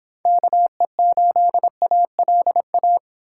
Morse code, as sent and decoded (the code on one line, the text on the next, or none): -.- . ---.. .- .-.. .-
KE8ALA